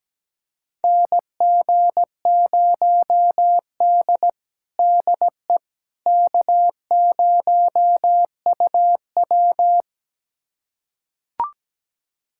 Morse code, heard as a keyed tone, about 17 wpm, 700 Hz. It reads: NG0D DE K0UW
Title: USA Callsign DE Callsign - 4 Characters - 17wpm